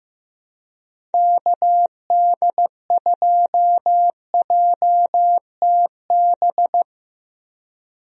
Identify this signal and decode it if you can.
Morse code, heard as a keyed tone, about 16 wpm, 700 Hz. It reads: KD2JTB
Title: USA Call Signs - 6 Characters - Repeated 3x - 15wpm